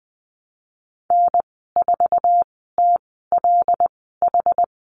Morse code, heard as a keyed tone, about 20 wpm, spaced as for 14 wpm, 700 Hz. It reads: N4TLH
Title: 140 Lesson 23g - Letter f - Call Signs 20wpm